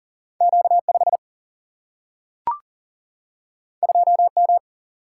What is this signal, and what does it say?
Morse code, heard as a keyed tone, about 40 wpm, 700 Hz. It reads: Q5 2M
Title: Single-Letter Single-Number 40wpm